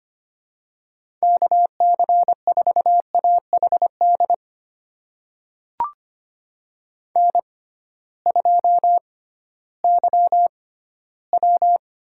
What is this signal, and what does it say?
Morse code, heard as a keyed tone, about 25 wpm, spaced as for 15 wpm, 700 Hz. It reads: KC4AHD N2YW
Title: USA Call Signs - 6x Character Spacing - 25wpm